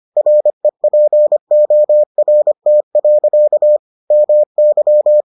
Morse code, heard as a keyed tone, about 25 wpm, 600 Hz. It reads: REPORT. MY